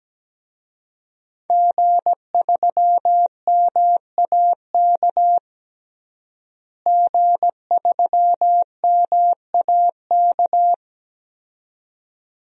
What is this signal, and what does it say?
Morse code, heard as a keyed tone, about 17 wpm, 700 Hz. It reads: G3MAK G3MAK
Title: DXCC-100 Call Signs - 5 Characters - Repeated 2x - 17wpm